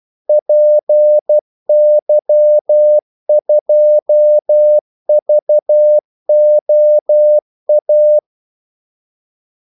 Morse code, heard as a keyed tone, about 12 wpm, 600 Hz. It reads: PY2VOA
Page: 2 ekstra ordmellemrum mellem kaldesignalerne